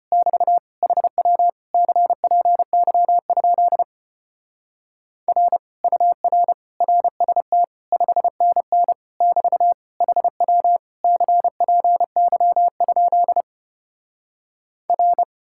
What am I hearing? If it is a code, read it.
Morse code, heard as a keyed tone, about 33 wpm, 700 Hz. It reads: <BT> HW CPY? R UR RST 5NN <BT> HW CPY? R